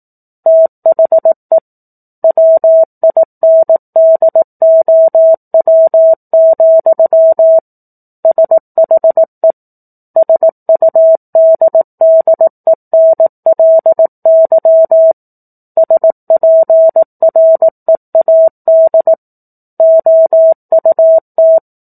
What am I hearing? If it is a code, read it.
Morse code, heard as a keyed tone, about 18 wpm, 650 Hz. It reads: THE WINDOW, SHE SUDDENLY SPREAD OUT
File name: 04-Chapter-04 18 WPM